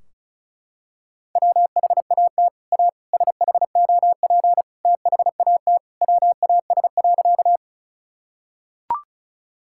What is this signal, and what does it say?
Morse code, heard as a keyed tone, about 36 wpm, 700 Hz. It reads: WHAT A SHOP THAT WAS.
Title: Sentences from Top 1600 Words 35wpm